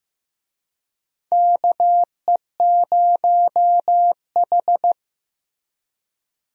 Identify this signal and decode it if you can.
Morse code, heard as a keyed tone, about 16 wpm, 700 Hz. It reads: KE0H
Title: USA Call Signs - 4 Characters - Repeated 2x - 15wpm